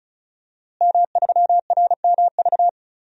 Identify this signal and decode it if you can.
Morse code, heard as a keyed tone, about 36 wpm, 700 Hz. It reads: M3RMV
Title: DXCC-100 Call Signs - 5 Characters - Repeated 2x - 35wpm